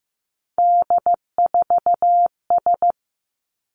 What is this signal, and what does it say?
Morse code, heard as a keyed tone, about 15 wpm, 700 Hz. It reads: D4S
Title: Three Challenging Letter-Number Combinations 15wpm